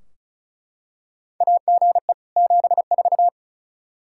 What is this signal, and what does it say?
Morse code, heard as a keyed tone, about 36 wpm, 700 Hz. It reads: AGE 74
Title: Practice QSO - Exchange 3 - More Info - 35wpm